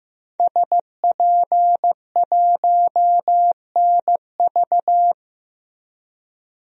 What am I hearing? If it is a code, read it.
Morse code, heard as a keyed tone, about 15 wpm, 700 Hz. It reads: SP1NV
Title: DXCC-100 Call Signs - 5 Characters - Repeated 3x - 15wpm